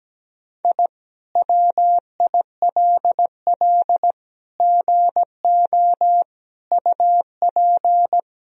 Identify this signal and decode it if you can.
Morse code, heard as a keyed tone, about 17 wpm, 700 Hz. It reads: I WILL GO UP
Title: Sentences from Top 1000 Words - Review Part 3 - 17wpm